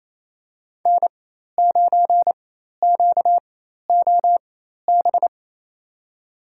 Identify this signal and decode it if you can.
Morse code, heard as a keyed tone, about 28 wpm, spaced as for 13 wpm, 700 Hz. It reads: N9QOB
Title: USA Call Signs - 5 Characters - 4x Character Spacing - 28wpm